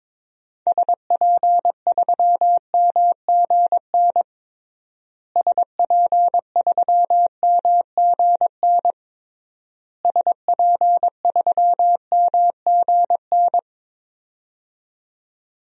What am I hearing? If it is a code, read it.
Morse code, heard as a keyed tone, about 22 wpm, 700 Hz. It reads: SP3MGN SP3MGN SP3MGN
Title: DXCC-100 Call Signs - 6 Characters - Repeated 3x - 22wpm